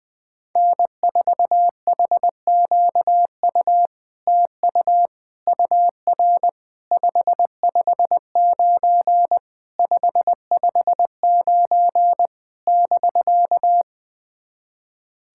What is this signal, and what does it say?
Morse code, heard as a keyed tone, about 20 wpm, 700 Hz. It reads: N4HQU TU UR 559 559 <BK>
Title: SOTA QSO Practice 20wpm